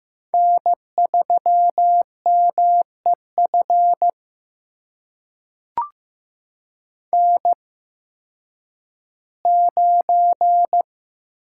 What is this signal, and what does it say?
Morse code, heard as a keyed tone, about 15 wpm, spaced as for 10 wpm, 700 Hz. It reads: N3MEF N9
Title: USA Call Signs - 5 Characters - 8x Character Spacing - 15wpm